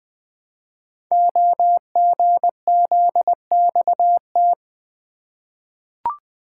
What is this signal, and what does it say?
Morse code, heard as a keyed tone, about 20 wpm, 700 Hz. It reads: OGZXT
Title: Code Groups - 5 Letters - 20wpm